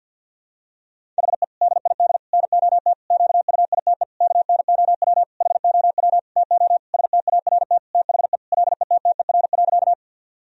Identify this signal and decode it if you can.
Morse code, heard as a keyed tone, about 52 wpm, 700 Hz. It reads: HE DID NOT QUITE KNOW HOW TO START THE LETTER.
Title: Sentences from Top 1000 Words - Review Part 1 - 50wpm